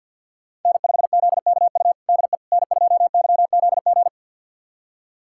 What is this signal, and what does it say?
Morse code, heard as a keyed tone, about 46 wpm, 700 Hz. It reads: N5ZKU DE N2YZG
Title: USA Callsign DE Callsign - 5 Characters - 50wpm